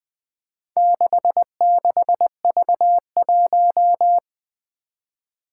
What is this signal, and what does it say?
Morse code, heard as a keyed tone, about 20 wpm, 700 Hz. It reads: 66V1
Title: Four Challenging Letter-Number Combinations 20wpm